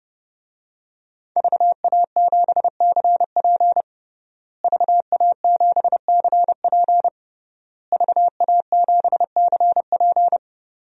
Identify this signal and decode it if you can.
Morse code, heard as a keyed tone, about 30 wpm, 700 Hz. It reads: VA7CP VA7CP VA7CP
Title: DXCC-100 Call Signs - 5 Characters - Repeated 3x - 30wpm